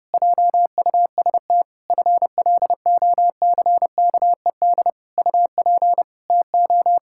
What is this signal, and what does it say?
Morse code, heard as a keyed tone, about 30 wpm, 700 Hz. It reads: JUST FLOCKED UP TO
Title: Pangram Challenge v2 - 30wpm